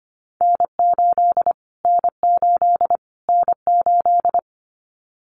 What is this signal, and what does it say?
Morse code, heard as a keyed tone, about 25 wpm, 700 Hz. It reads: N8 N8 N8
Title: Callsign Prefixes for Top 25 DXCC - Repeated 3x - 25wpm